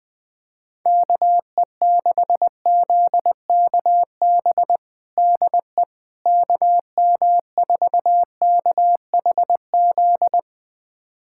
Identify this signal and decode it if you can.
Morse code, heard as a keyed tone, about 21 wpm, 700 Hz. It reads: KE6ZKB DE KM4KHZ
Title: USA Callsign DE Callsign 20wpm